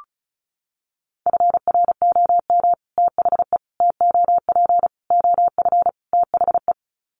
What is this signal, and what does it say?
Morse code, heard as a keyed tone, about 34 wpm, 700 Hz. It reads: FROM THE TOP OF THE
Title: Sets of 5 Words Encore 35wpm